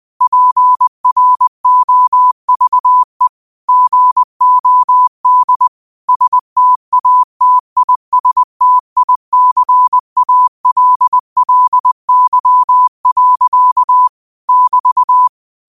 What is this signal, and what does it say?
Morse code, heard as a keyed tone, about 20 wpm, 1000 Hz. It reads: PROVE GOD STATISTICALLY. <BT>
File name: qotdfiles / 20wpm